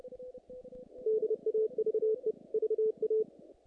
Morse code, heard as a keyed tone, about 30 wpm, 440 Hz. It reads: DAVE VA
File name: contests / naqp-cw-k5tr-op-k5pi- -2000